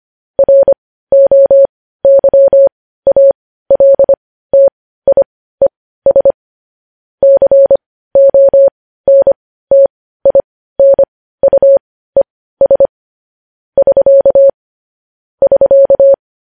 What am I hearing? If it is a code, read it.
Morse code, heard as a keyed tone, about 25 wpm, spaced as for 15 wpm, 550 Hz. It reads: ROYALTIES CONTINUES <SK> <SK>
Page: news-headlines-in-morse-code-at-15-wpm